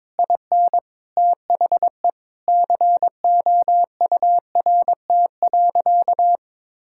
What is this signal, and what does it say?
Morse code, heard as a keyed tone, about 22 wpm, 700 Hz. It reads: IN THE COURT.
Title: Sentences from Top 1600 Words 22wpm